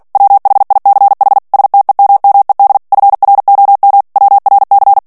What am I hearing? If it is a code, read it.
Morse code, heard as a keyed tone, about 48 wpm, 800 Hz. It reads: WHICH STEMMED FROM WRC
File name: ANGLAIS texte clair 3 vitesse 45 mots minute